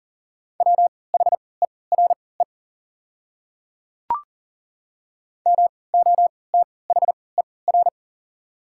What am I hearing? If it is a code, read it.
Morse code, heard as a keyed tone, about 38 wpm, spaced as for 22 wpm, 690 Hz. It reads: WHERE MOTHER
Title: 3x Character Spacing - Top 200 Words - 40wpm